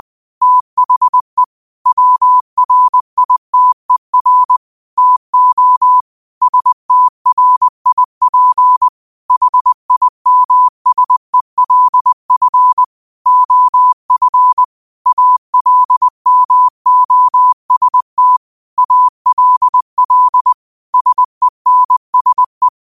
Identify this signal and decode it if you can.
Morse code, heard as a keyed tone, about 20 wpm, 1000 Hz. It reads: THE WRITER TO STRIP HIMSELF OF ALMOST ALL SENSE
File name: qotdfiles / 20wpm